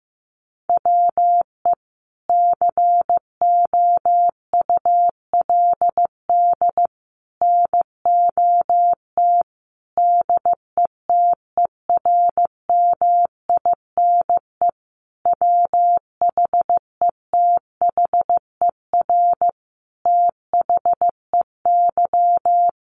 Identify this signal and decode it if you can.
Morse code, heard as a keyed tone, about 15 wpm, 700 Hz. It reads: WE COULD NOT DETERMINE WHETHER THEY
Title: Sentences from Top 1300 Words 15wpm